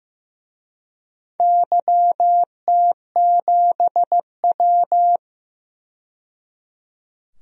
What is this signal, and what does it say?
Morse code, heard as a keyed tone, about 15 wpm, 700 Hz. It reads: YT7W
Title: DXCC-100 Call Signs - 4 Characters - 15wpm